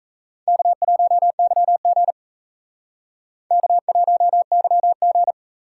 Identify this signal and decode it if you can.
Morse code, heard as a keyed tone, about 34 wpm, 700 Hz. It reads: K1YG K1YG
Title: Speed Racing - USA 1x2 Call Signs 28wpm